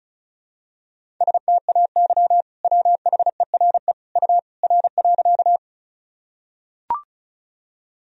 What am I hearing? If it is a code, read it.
Morse code, heard as a keyed tone, about 36 wpm, 700 Hz. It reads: STAY WHERE U R.
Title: Sentences from Top 500 Words and Abbreviations 35wpm